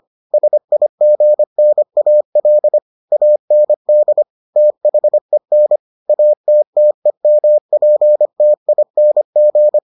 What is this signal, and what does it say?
Morse code, heard as a keyed tone, about 25 wpm, 600 Hz. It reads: SIGNAL AND THEN ATTEMPTING